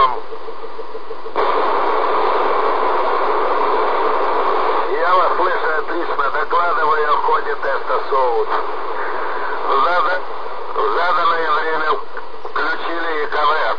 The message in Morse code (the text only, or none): none